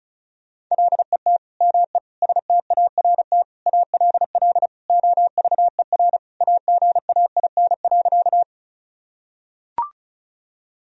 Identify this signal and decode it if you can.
Morse code, heard as a keyed tone, about 36 wpm, 700 Hz. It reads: LET ME START ALL OVER AGAIN.
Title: Sentences from Top 1000 Words - Review Part 1 - 35wpm